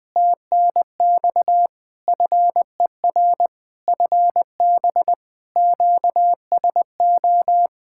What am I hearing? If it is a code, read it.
Morse code, heard as a keyed tone, about 20 wpm, 700 Hz. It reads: TNX FER FB QSO
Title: Practice QSO - Exchange 4 - Ending - 20wpm